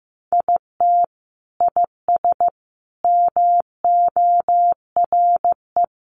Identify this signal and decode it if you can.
Morse code, heard as a keyed tone, about 15 wpm, 700 Hz. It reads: IT IS MORE
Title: Sentences from Top 1400 Words 15wpm